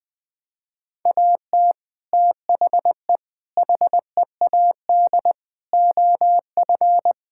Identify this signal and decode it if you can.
Morse code, heard as a keyed tone, about 20 wpm, 700 Hz. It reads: AT THE HEAD OF
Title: Build A Phrase - 5-Word Phrases - 20wpm